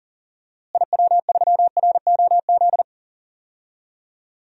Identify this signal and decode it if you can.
Morse code, heard as a keyed tone, about 40 wpm, 700 Hz. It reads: IW3ROZ
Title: DXCC-100 Call Signs - 6 Characters - Repeated 2x - 40wpm